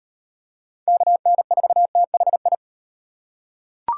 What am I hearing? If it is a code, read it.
Morse code, heard as a keyed tone, about 38 wpm, 700 Hz. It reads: KN4THI